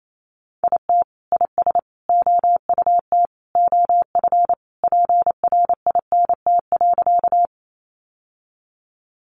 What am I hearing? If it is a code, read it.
Morse code, heard as a keyed tone, about 28 wpm, 700 Hz. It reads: IT IS OUT OF PRINT.
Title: Sentences from Top 1400 Words 28wpm